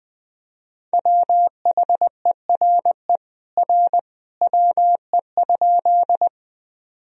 Morse code, heard as a keyed tone, about 20 wpm, 700 Hz. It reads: WHERE R WE?
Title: Sentences from Top 200 Words and Abbreviations 20wpm